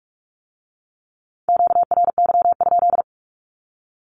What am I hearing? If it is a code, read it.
Morse code, heard as a keyed tone, about 46 wpm, 700 Hz. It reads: QRY?
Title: Q-Codes with Spoken Meaning - 45wpm